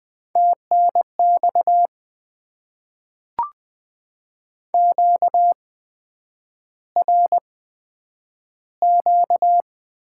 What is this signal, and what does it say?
Morse code, heard as a keyed tone, about 20 wpm, spaced as for 9 wpm, 700 Hz. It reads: TNX QRQ
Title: Top 90 QSO Elements - 8x Character Spacing - 20wpm